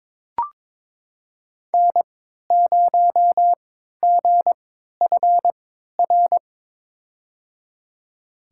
Morse code, heard as a keyed tone, about 21 wpm, spaced as for 12 wpm, 700 Hz. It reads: N0GFR
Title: USA Call Signs - 5 Characters - 3x Character Spacing - 22wpm